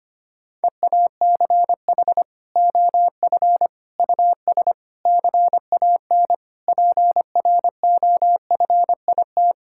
Morse code, heard as a keyed tone, about 25 wpm, 700 Hz. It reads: EACH OF US CAN PROFIT
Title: Sentences from Top 1000 Words - Review Part 5 - 25wpm